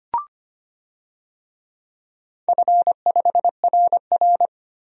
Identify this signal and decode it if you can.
Morse code, heard as a keyed tone, about 25 wpm, 700 Hz. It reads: F5RR
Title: DXCC-100 Call Signs - 4 Characters - 25wpm